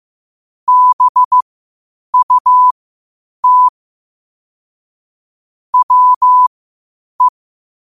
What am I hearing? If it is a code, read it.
Morse code, heard as a keyed tone, about 15 wpm, spaced as for 8 wpm, 1000 Hz. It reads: BUT WE
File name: qotdfiles / 8wpm